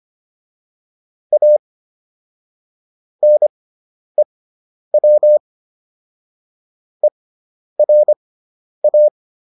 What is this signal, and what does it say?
Morse code, heard as a keyed tone, about 25 wpm, spaced as for 10 wpm, 600 Hz. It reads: A NEW ERA